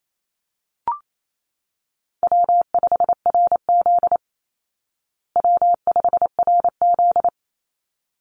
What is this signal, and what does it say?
Morse code, heard as a keyed tone, about 28 wpm, 700 Hz. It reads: W5RZ W5RZ
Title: Call Signs Repeated 3x - 28wpm